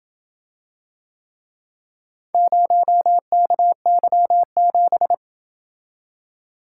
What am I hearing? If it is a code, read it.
Morse code, heard as a keyed tone, about 27 wpm, 700 Hz. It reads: 0KY7